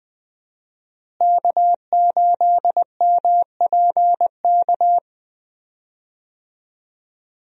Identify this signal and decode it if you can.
Morse code, heard as a keyed tone, about 20 wpm, 700 Hz. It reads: K8MPK
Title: USA Call Signs - 5 Characters - Repeated 2x - 20wpm